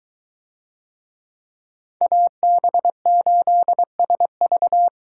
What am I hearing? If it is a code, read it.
Morse code, heard as a keyed tone, about 23 wpm, 700 Hz. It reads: AB8SV